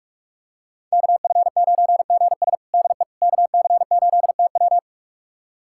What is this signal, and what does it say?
Morse code, heard as a keyed tone, about 41 wpm, 700 Hz. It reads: KF9GS DE KC8TW
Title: USA Callsign DE Callsign - 5 Characters - 45wpm